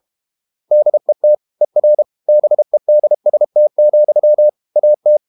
Morse code, heard as a keyed tone, about 32 wpm, 600 Hz. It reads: DET ER BEDST, AT
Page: Ferie 32 wpm